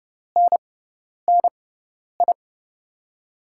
Morse code, heard as a keyed tone, about 32 wpm, spaced as for 10 wpm, 700 Hz. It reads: NNI
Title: USA Call Signs - 6 Characters - 6x Character Spacing - 30wpm